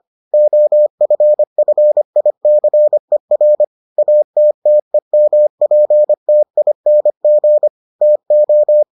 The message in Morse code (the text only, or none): OFFICER ATTEMPTING TO